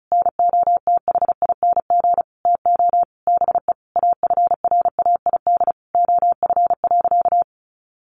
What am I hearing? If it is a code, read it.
Morse code, heard as a keyed tone, about 34 wpm, 700 Hz. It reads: NOTHING TO BE AFRAID OF.
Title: Sentences from Top 1500 Words 35wpm